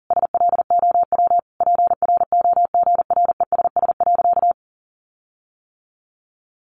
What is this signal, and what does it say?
Morse code, heard as a keyed tone, about 40 wpm, 700 Hz. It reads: SLOW PROGRESS.
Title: Sentences from Top 1400 Words 40wpm